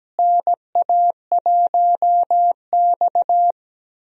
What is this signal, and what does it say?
Morse code, heard as a keyed tone, about 17 wpm, 700 Hz. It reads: NA1X
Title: USA Call Signs - 4 Characters - Repeated 2x - 17wpm